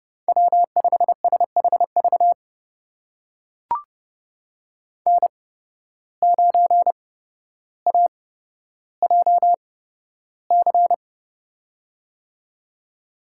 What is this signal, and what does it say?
Morse code, heard as a keyed tone, about 30 wpm, spaced as for 13 wpm, 700 Hz. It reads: W5SHV N9AJC
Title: USA Call Signs - 5 Characters - 8x Character Spacing - 30wpm